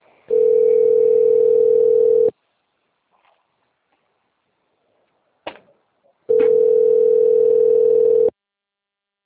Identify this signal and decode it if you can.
Ringback tone (North America), 440 + 480 Hz.
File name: rg-1600-unknown- - - .79702